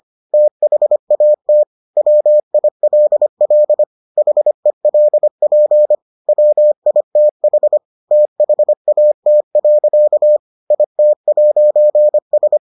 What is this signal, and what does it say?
Morse code, heard as a keyed tone, about 25 wpm, 600 Hz. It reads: THAT WILL HELP WITH THAT. IT'S